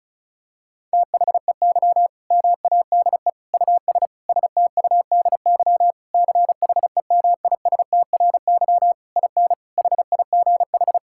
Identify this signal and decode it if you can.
Morse code, heard as a keyed tone, about 36 wpm, 700 Hz. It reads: THEY MADE US STUDY CHEMISTRY IN HIGH
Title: Sentences from Top 1000 Words - Review Part 6 - 35wpm